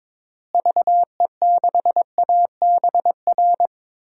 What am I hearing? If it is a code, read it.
Morse code, heard as a keyed tone, about 22 wpm, 700 Hz. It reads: VE6ABR
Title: DXCC-100 Call Signs - 6 Characters - Repeated 2x - 22wpm